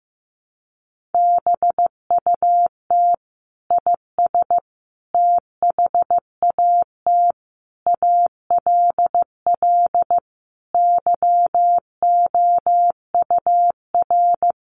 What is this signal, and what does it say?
Morse code, heard as a keyed tone, about 15 wpm, 700 Hz. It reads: BUT IS THAT ALL YOUR
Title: Sentences from Top 1300 Words 15wpm